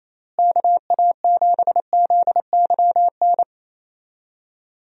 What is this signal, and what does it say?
Morse code, heard as a keyed tone, about 28 wpm, 700 Hz. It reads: KA7ZYN
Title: USA Call Signs - 6 Characters - Repeated 2x - 28wpm